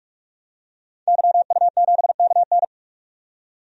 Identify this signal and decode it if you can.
Morse code, heard as a keyed tone, about 44 wpm, 690 Hz. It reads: YU7KN
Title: DXCC-100 Call Signs - 5 Characters - 45wpm